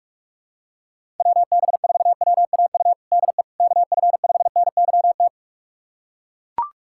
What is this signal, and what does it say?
Morse code, heard as a keyed tone, about 41 wpm, 700 Hz. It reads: WB4WAU DE KF5NYT
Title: USA Callsign DE Callsign - 6 Characters - 45wpm